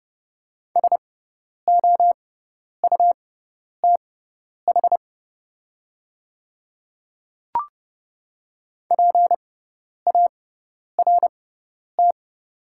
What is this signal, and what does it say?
Morse code, heard as a keyed tone, about 30 wpm, spaced as for 10 wpm, 700 Hz. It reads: SOUTH PART
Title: 6x Character Spacing - Top 500 Words - 30wpm